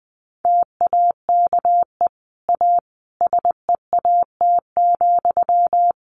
Morse code, heard as a keyed tone, about 20 wpm, 700 Hz. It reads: TAKE A SEAT,
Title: Sentences from Top 1500 Words 20wpm